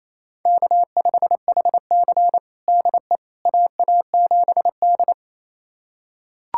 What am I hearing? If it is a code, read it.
Morse code, heard as a keyed tone, about 28 wpm, 700 Hz. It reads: K5HC DE AA7D
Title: USA Callsign DE Callsign - 4 Characters - 28wpm